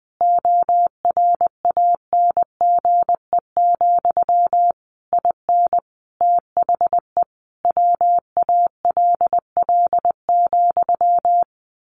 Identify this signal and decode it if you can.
Morse code, heard as a keyed tone, about 20 wpm, 700 Hz. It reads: ORANGE, IN THE WALL,
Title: A Sound Of Thunder 20wpm